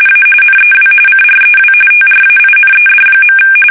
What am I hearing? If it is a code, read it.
RTTY, about 45.45 baud. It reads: A NEW CREW TRAINED.
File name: fsk850 / battleshipnj